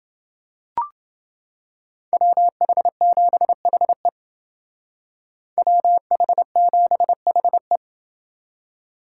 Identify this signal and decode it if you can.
Morse code, heard as a keyed tone, about 29 wpm, 700 Hz. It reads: WH7HE WH7HE